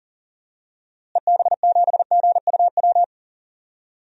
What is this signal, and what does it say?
Morse code, heard as a keyed tone, about 37 wpm, 700 Hz. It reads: EB7GUW
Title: DXCC-100 Call Signs - 6 Characters - Repeated 2x - 40wpm